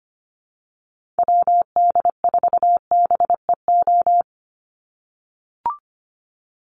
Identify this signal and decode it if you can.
Morse code, heard as a keyed tone, about 25 wpm, 700 Hz. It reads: WD4BEO
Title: USA Call Signs 25wpm